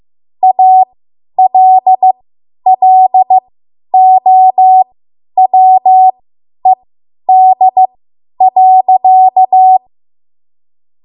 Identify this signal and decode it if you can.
Morse code, heard as a keyed tone, about 15 wpm, spaced as for 10 wpm, 750 Hz. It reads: ALLOWED.